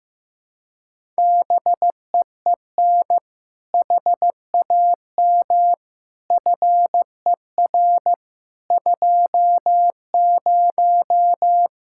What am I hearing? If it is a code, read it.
Morse code, heard as a keyed tone, about 15 wpm, 700 Hz. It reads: BEEN HAM FER 20
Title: Practice QSO - Exchange 3 - More Info - 15wpm